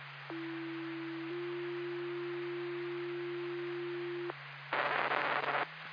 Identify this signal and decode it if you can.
Two-tone page, heard as a one-second tone, then a three-second tone, about 330 then 350 Hz, with noise. It is heard from 0.3 to 4.3 s.